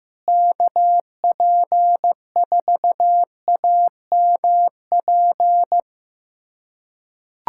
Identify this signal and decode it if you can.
Morse code, heard as a keyed tone, about 15 wpm, 700 Hz. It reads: KP4AMP
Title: USA Callsign DE Callsign 15wpm